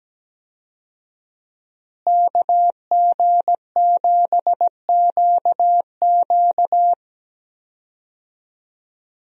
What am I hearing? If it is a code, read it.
Morse code, heard as a keyed tone, about 17 wpm, 700 Hz. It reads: KG7QQ
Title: USA Call Signs - 5 Characters - Repeated 2x - 17wpm